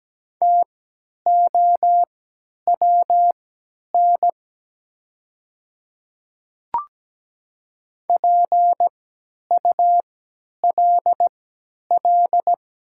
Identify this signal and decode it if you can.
Morse code, heard as a keyed tone, about 17 wpm, spaced as for 10 wpm, 700 Hz. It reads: TOWN PULL
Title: 3x Character Spacing - Top 500 Words - 17wpm